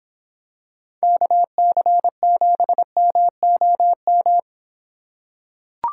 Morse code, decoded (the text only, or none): KC7MOM